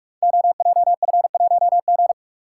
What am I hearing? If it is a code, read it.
Morse code, heard as a keyed tone, about 44 wpm, 700 Hz. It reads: OJF1G
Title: Code Groups - 5 Letter:Numbers - 45wpm